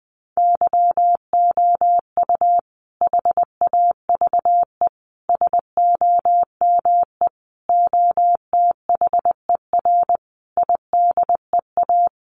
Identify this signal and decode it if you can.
Morse code, heard as a keyed tone, about 20 wpm, 700 Hz. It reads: YOU HAVE SOME OTHER IDEA
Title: Sentences from Top 1300 Words 20wpm